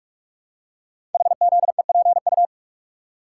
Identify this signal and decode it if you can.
Morse code, heard as a keyed tone, about 44 wpm, 700 Hz. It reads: HZEPU
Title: Code Groups - 5 Letters - 45wpm